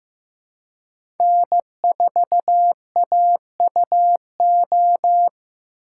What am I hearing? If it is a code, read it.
Morse code, heard as a keyed tone, about 15 wpm, 700 Hz. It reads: N4AUO
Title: USA Call Signs - 5 Characters - 2x Character Spacing - 15wpm